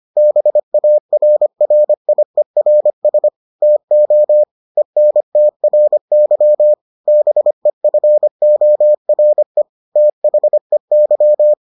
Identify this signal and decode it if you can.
Morse code, heard as a keyed tone, about 25 wpm, 600 Hz. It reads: BARRIERS TO ENTRY BEFORE THEY